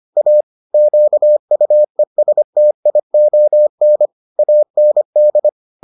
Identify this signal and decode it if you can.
Morse code, heard as a keyed tone, about 25 wpm, 600 Hz. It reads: A QUESTION AND